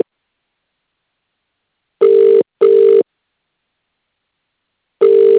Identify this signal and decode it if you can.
Ringback tone (UK), 400 + 450 Hz.